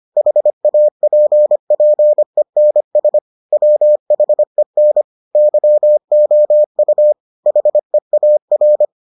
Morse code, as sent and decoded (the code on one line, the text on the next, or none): .... .- .--. .--. -. ... / .-- .... -. / -.-- --- ..- / .... .- .-.
HAPPENS WHEN YOU HEAR